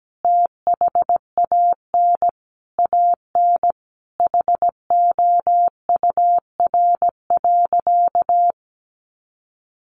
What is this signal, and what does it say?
Morse code, heard as a keyed tone, about 17 wpm, 700 Hz. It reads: THAN AN HOUR.
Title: Sentences from Top 500 Words 17wpm